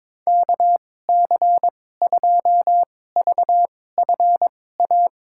Morse code, heard as a keyed tone, about 23 wpm, spaced as for 16 wpm, 700 Hz. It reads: KC2VFA
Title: USA Call Signs - 6 Characters - 2x Character Spacing - 22wpm